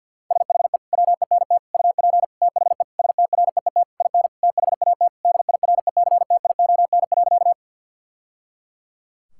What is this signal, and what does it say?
Morse code, heard as a keyed tone, about 50 wpm, 700 Hz. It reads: SHE WENT UP THE STREET IN THAT DIRECTION.
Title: Sentences from Top 1000 Words - Review Part 5 - 50wpm